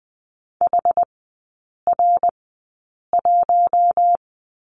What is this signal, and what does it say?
Morse code, heard as a keyed tone, about 20 wpm, spaced as for 8 wpm, 700 Hz. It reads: HR1
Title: 114 Lesson 19e - Letter w - Three Characters 20wpm